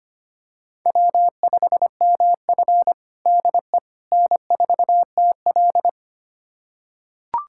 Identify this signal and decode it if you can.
Morse code, heard as a keyed tone, about 25 wpm, 700 Hz. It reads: W5MF DE N4TL
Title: USA Callsign DE Callsign - 4 Characters - 25wpm